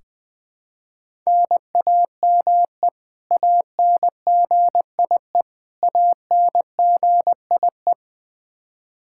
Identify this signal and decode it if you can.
Morse code, heard as a keyed tone, about 21 wpm, 700 Hz. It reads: NAME ANGIE ANGIE
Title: Practice QSO - Exchange 1 - Beginning - 20wpm